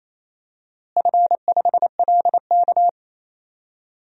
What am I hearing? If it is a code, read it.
Morse code, heard as a keyed tone, about 27 wpm, 700 Hz. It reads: F5LK
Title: DXCC-100 Call Signs - 4 Characters - Repeated 3x - 28wpm